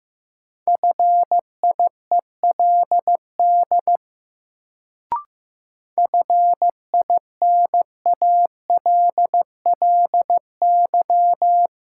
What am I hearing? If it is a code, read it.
Morse code, heard as a keyed tone, about 15 wpm, 700 Hz. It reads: FIELD FINALLY
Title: Sentences from Top 1000 Words - New Word Drill - 15wpm